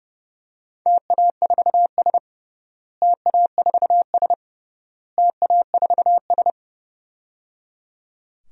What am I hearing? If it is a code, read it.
Morse code, heard as a keyed tone, about 30 wpm, 700 Hz. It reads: TA4S TA4S TA4S
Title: DXCC-100 Call Signs - 4 Characters - Repeated 3x - 30wpm